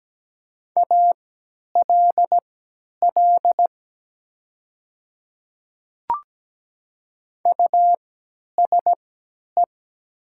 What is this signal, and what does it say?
Morse code, heard as a keyed tone, about 17 wpm, spaced as for 10 wpm, 700 Hz. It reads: ALL USE
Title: 3x Character Spacing - Top 300 Words - 17wpm